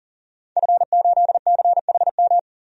Morse code, heard as a keyed tone, about 40 wpm, 700 Hz. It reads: F8CHM
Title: DXCC-100 Call Signs - 5 Characters - 40wpm